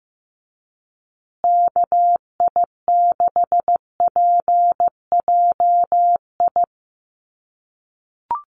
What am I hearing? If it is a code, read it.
Morse code, heard as a keyed tone, about 15 wpm, 700 Hz. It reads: KI6PJI
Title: USA Call Signs Repeated 3x 15wpm